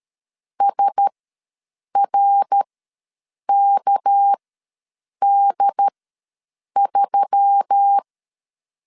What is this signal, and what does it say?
Morse code, heard as a keyed tone, about 13 wpm, spaced as for 7 wpm, 800 Hz. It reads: SRKD3
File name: LEZIONI / 6 PKX0 / 6 PK ws 12wpm 3116